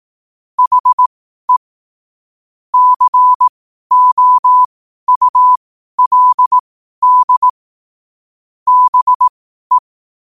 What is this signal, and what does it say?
Morse code, heard as a keyed tone, about 18 wpm, spaced as for 12 wpm, 1000 Hz. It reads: HE COULD BE
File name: qotdfiles / 12wpm